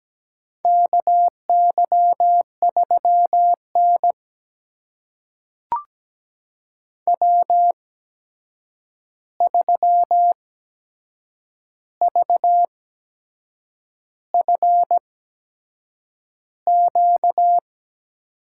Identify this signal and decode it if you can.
Morse code, heard as a keyed tone, about 17 wpm, spaced as for 7 wpm, 700 Hz. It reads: KY3N W3VFQ
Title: USA Call Signs - 8x Character Spacing - 17wpm